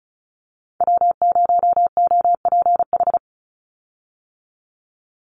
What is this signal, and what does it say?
Morse code, heard as a keyed tone, about 34 wpm, 700 Hz. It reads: W0OPH
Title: USA Call Signs 35wpm